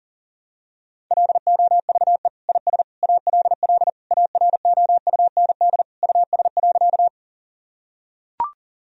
Morse code, heard as a keyed tone, about 40 wpm, 700 Hz. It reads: LOVE IS ALL AROUND US.
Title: Sentences from Top 1000 Words - Review Part 3 - 40wpm